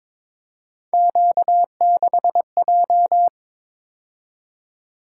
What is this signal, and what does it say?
Morse code, heard as a keyed tone, about 26 wpm, 700 Hz. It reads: Q6J
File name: Speed-Racing - Three Letter-Number Combinations - 22wpm